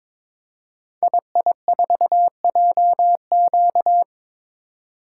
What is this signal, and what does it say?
Morse code, heard as a keyed tone, about 22 wpm, 700 Hz. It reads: II4JQ
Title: Code Groups - 5 Letter:Numbers - 22wpm